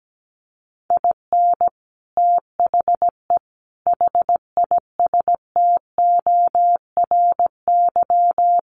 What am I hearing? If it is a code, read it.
Morse code, heard as a keyed tone, about 17 wpm, 700 Hz. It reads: IN THE HISTORY
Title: Sets of 4 Words 17wpm